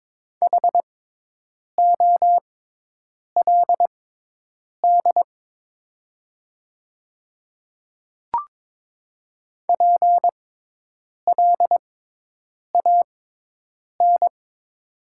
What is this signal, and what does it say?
Morse code, heard as a keyed tone, about 23 wpm, spaced as for 8 wpm, 700 Hz. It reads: HOLD PLAN
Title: 6x Character Spacing - Top 500 Words - 22wpm